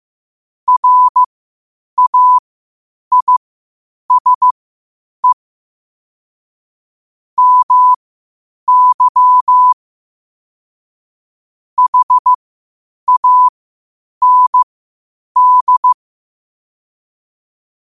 Morse code, heard as a keyed tone, about 15 wpm, spaced as for 8 wpm, 1000 Hz. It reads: RAISE MY HAND
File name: qotdfiles / 8wpm